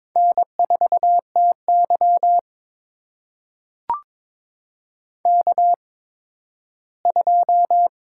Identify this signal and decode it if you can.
Morse code, heard as a keyed tone, about 22 wpm, spaced as for 13 wpm, 700 Hz. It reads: N4TY K2
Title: USA Call Signs - 4 Characters - 8x Character Spacing - 22wpm